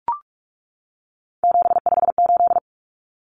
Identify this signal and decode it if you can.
Morse code, heard as a keyed tone, about 44 wpm, 700 Hz. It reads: 758
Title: Common Cut Numbers - 3-Digit 45wpm